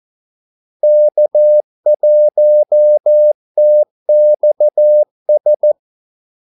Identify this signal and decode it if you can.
Morse code, heard as a keyed tone, about 14 wpm, 600 Hz. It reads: K1TXS
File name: Callsigns-14wpm